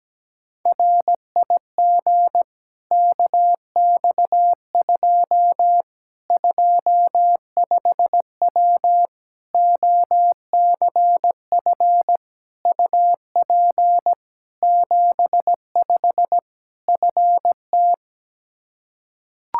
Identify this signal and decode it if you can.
Morse code, heard as a keyed tone, about 17 wpm, 700 Hz. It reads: RIG KX2 25W OCF UP 75 FT
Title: K1USN SST QSO Practice 17wpm